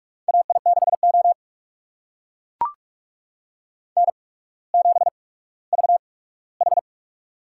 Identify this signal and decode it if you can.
Morse code, heard as a keyed tone, about 44 wpm, spaced as for 19 wpm, 690 Hz. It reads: AI6O N7VH
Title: USA Call Signs - 4 Characters - 8x Character Spacing - 45wpm